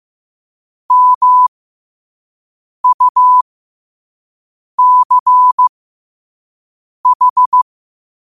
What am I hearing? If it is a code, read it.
Morse code, heard as a keyed tone, about 15 wpm, spaced as for 5 wpm, 1000 Hz. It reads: MUCH
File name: qotdfiles / 5wpm